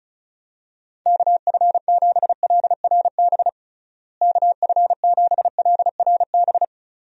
Morse code, heard as a keyed tone, about 34 wpm, 700 Hz. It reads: KF7LRB KF7LRB
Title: USA Call Signs - 6 Characters - Repeated 2x - 35wpm